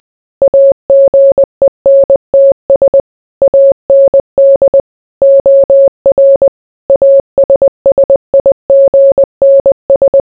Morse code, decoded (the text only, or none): AGENTS AND OR ASSIGNS